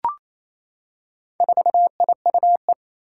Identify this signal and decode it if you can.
Morse code, heard as a keyed tone, about 27 wpm, 700 Hz. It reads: 4IUE
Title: Four Challenging Letter-Number Combinations 28wpm